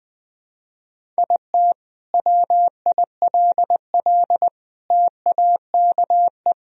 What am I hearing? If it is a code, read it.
Morse code, heard as a keyed tone, about 20 wpm, 700 Hz. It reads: IT WILL TAKE
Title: Sentences from Top 1000 Words - Review Part 1 - 20wpm